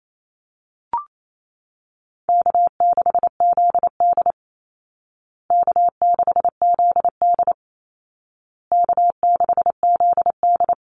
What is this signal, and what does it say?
Morse code, heard as a keyed tone, about 28 wpm, 700 Hz. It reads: K6ZD K6ZD K6ZD
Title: Call Signs Repeated 2x - 28wpm